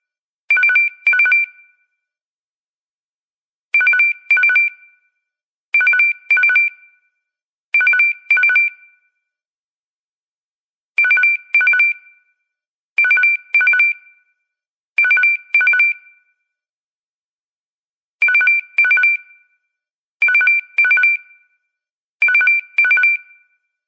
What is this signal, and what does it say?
There is no code.